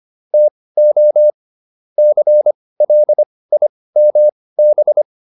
Morse code, heard as a keyed tone, about 25 wpm, spaced as for 18 wpm, 600 Hz. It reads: TO CLIMB